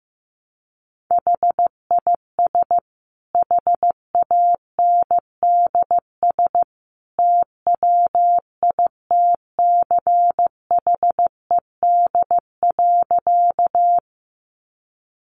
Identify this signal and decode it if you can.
Morse code, heard as a keyed tone, about 14 wpm, 700 Hz. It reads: HIS HANDS TWITCHED.
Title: A Sound Of Thunder 15wpm